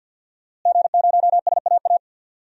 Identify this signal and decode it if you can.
Morse code, heard as a keyed tone, about 52 wpm, 700 Hz. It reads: G0SAA
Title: DXCC-100 Call Signs - 5 Characters - Repeated 2x - 50wpm